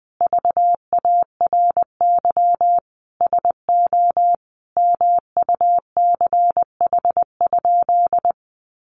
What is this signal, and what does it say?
Morse code, heard as a keyed tone, about 20 wpm, 700 Hz. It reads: VARY SO MUCH?
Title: Sentences from Top 1200 Words 20wpm